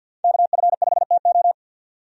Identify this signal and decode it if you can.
Morse code, heard as a keyed tone, about 53 wpm, 700 Hz. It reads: KF5TO
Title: USA Call Signs - 5 Characters - 50wpm